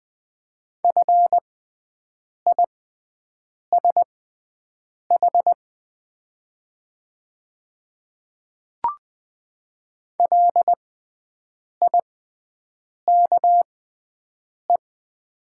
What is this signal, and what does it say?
Morse code, heard as a keyed tone, about 20 wpm, spaced as for 7 wpm, 700 Hz. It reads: FISH LIKE
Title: 6x Character Spacing - Top 300 Words - 20wpm